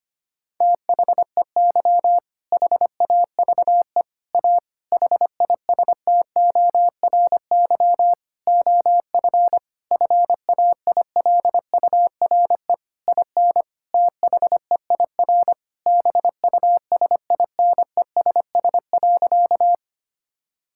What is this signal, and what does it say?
Morse code, heard as a keyed tone, about 25 wpm, 700 Hz. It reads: THEY HAVE A HISTORY OF FAILURE IN THEIR BUSINESS.
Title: Sentences from Top 1000 Words - Review Part 6 - 25wpm